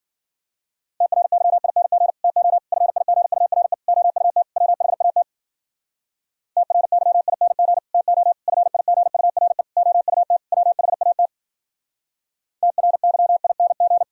Morse code, heard as a keyed tone, about 52 wpm, 700 Hz. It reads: TRYING TO FIGURE OUT WHAT TRYING TO FIGURE OUT WHAT TRYING